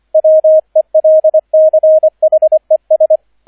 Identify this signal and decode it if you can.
Morse code, heard as a keyed tone, about 25 wpm, 630 Hz, with radio noise. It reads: WELCHES
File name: 003 Dritter Streich 25wpm